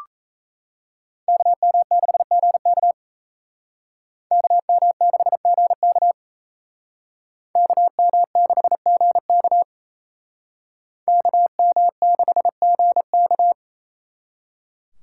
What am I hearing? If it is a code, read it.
Morse code, heard as a keyed tone, about 36 wpm, 700 Hz. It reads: KM6GK KM6GK KM6GK KM6GK